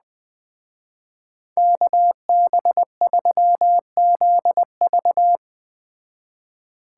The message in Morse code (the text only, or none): KB3ZV